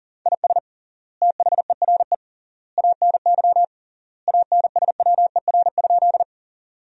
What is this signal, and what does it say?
Morse code, heard as a keyed tone, about 40 wpm, 700 Hz. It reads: IS THERE ANY ANSWER?
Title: 3x Word-Spacing - Sentences from Top 300 Words - 40wpm